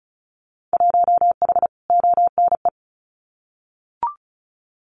Morse code, heard as a keyed tone, about 34 wpm, 700 Hz. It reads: 1H ONE
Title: Winter Field Day Exchanges Encore 35wpm